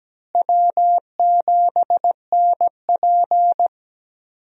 Morse code, heard as a keyed tone, about 17 wpm, 700 Hz. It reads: W7NP
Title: USA Call Signs - 4 Characters - Repeated 3x - 17wpm